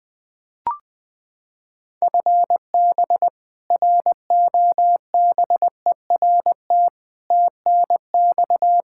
Morse code, heard as a keyed tone, about 20 wpm, 700 Hz. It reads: FB ROBERT TNX
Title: K1USN SST QSO Practice 20wpm